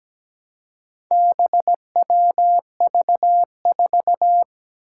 Morse code, heard as a keyed tone, about 17 wpm, 700 Hz. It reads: BWV4
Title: Four Challenging Letter-Number Combinations 17wpm